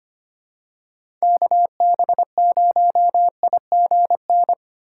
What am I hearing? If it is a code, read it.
Morse code, heard as a keyed tone, about 25 wpm, 700 Hz. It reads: KB0IGN
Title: USA Call Signs - 6 Characters - 2x Character Spacing - 25wpm